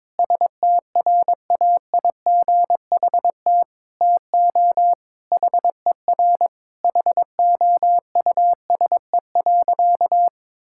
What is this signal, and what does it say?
Morse code, heard as a keyed tone, about 22 wpm, 700 Hz. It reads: STRAIGHT TO HER HOUSE.
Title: Sentences from Top 1000 Words - Review Part 6 - 22wpm